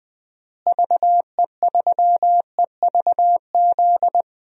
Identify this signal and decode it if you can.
Morse code, heard as a keyed tone, about 20 wpm, 700 Hz. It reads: VE3EVZ
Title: DXCC-100 Call Signs - 6 Characters - Repeated 3x - 20wpm